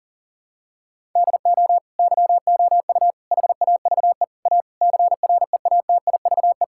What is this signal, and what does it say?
Morse code, heard as a keyed tone, about 40 wpm, 700 Hz. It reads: DO YOU HAVE A CREATIVE
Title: Sentences from Top 1000 Words - Review Part 5 - 40wpm